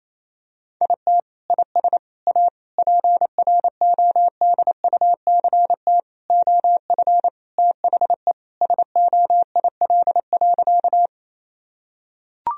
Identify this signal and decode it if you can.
Morse code, heard as a keyed tone, about 28 wpm, 700 Hz. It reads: IT IS A PRODUCT OF THE SOIL.
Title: Sentences from Top 1000 Words - Review Part 4 - 28wpm